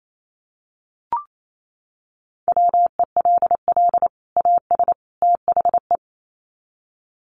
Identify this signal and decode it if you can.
Morse code, heard as a keyed tone, about 28 wpm, 700 Hz. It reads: WELL AS THE
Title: Sets of 3 Words 28wpm